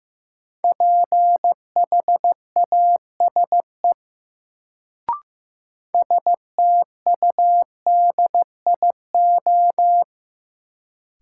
Morse code, heard as a keyed tone, about 15 wpm, 700 Hz. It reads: PHASE STUDIO
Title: Sentences from Top 1200 Words - New Word Drill - 15wpm